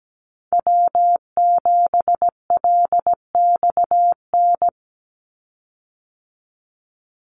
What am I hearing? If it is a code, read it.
Morse code, heard as a keyed tone, about 17 wpm, 700 Hz. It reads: W7LXN
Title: Call Signs Repeated 3x - 17wpm